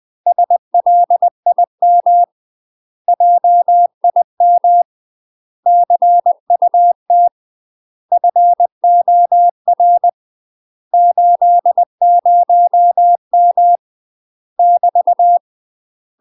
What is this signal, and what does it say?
Morse code, heard as a keyed tone, about 20 wpm, 700 Hz. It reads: SLIM JIM CUT FOR 80M <BT>
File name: UK MASTER QSOs TRAINING/20WPM